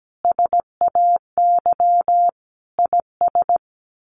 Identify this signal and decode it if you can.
Morse code, heard as a keyed tone, about 17 wpm, 700 Hz. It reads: SAY IS
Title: Sets of 5 Words Encore2 17wpm